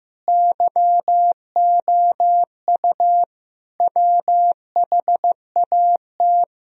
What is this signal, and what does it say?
Morse code, heard as a keyed tone, about 15 wpm, 700 Hz. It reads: YOU WHAT
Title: Sentences from Top 1000 Words - Review Part 1 - 15wpm